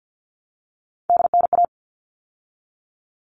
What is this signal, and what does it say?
Morse code, heard as a keyed tone, about 50 wpm, 700 Hz. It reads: DNA
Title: All Cut Numbers - 3-Digit 50wpm